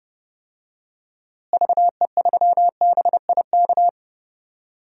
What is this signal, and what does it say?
Morse code, heard as a keyed tone, about 30 wpm, 700 Hz. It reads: VE3BIK
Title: DXCC-100 Call Signs - 6 Characters - Repeated 3x - 30wpm